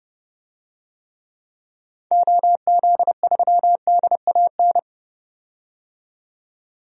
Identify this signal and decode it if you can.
Morse code, heard as a keyed tone, about 30 wpm, 700 Hz. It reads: OZ3DAN
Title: DXCC-100 Call Signs - 6 Characters - 30wpm